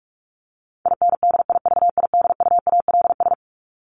Morse code, heard as a keyed tone, about 44 wpm, 700 Hz. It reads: INDIVIDUALS
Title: ES:S Suffix - Top 100 Words with Base Words 45wpm